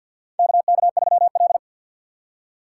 Morse code, heard as a keyed tone, about 51 wpm, 700 Hz. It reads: KK3L
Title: USA Call Signs - 4 Characters - 3x Character Spacing - 50wpm